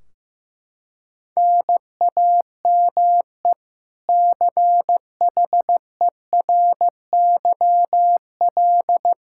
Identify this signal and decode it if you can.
Morse code, heard as a keyed tone, about 15 wpm, 700 Hz. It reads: NAME CHERYL
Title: K1USN SST QSO Practice 15wpm